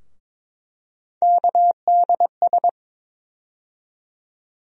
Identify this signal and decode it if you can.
Morse code, heard as a keyed tone, about 22 wpm, 700 Hz. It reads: KDS
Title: USA Call Sign Postfixes - 3 Letters 22wpm